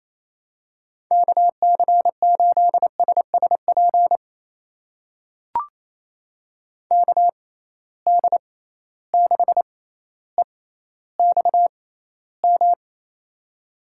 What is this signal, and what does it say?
Morse code, heard as a keyed tone, about 28 wpm, spaced as for 14 wpm, 700 Hz. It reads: KC8SSP KD6EXM
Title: USA Call Signs - 6 Characters - 6x Character Spacing - 28wpm